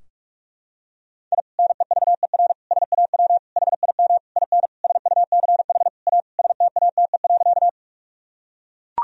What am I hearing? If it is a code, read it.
Morse code, heard as a keyed tone, about 42 wpm, 700 Hz. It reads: I NEVER SAW HIM IN SUCH A STATE.
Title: Sentences from Top 1000 Words - Review Part 2 - 45wpm